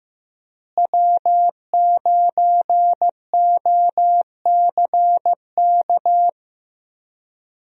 Morse code, heard as a keyed tone, about 15 wpm, 700 Hz. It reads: W9OCK
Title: USA Call Signs - 5 Characters - Repeated 2x - 15wpm